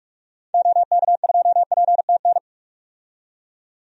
Morse code, heard as a keyed tone, about 46 wpm, 700 Hz. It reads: OK2PTN
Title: DXCC-100 Call Signs - 6 Characters - 45wpm